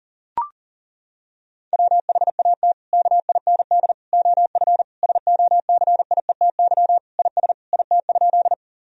Morse code, heard as a keyed tone, about 40 wpm, 700 Hz. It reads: WHAT KIND OF SOCIETY IS IT?
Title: Sentences from Top 1000 Words - Review Part 4 - 40wpm